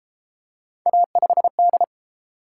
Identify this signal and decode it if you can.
Morse code, heard as a keyed tone, about 31 wpm, 700 Hz. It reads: A5D